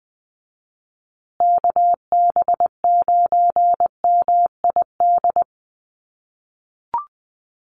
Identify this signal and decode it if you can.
Morse code, heard as a keyed tone, about 20 wpm, 700 Hz. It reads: KB9MID
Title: USA Call Signs Repeated 2x 20wpm